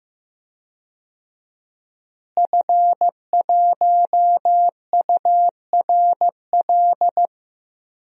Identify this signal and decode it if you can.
Morse code, heard as a keyed tone, about 15 wpm, 700 Hz. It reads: F1URL
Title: DXCC-100 Call Signs - 5 Characters - Repeated 3x - 15wpm